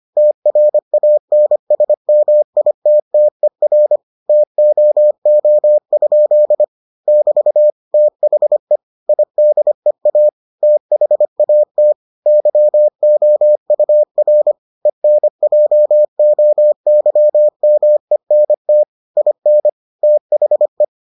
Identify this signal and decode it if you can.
Morse code, heard as a keyed tone, about 25 wpm, 600 Hz. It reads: TRANSMITTER TOO? <BT> THE IDEA THAT YOUR ENJOYMENT IN THE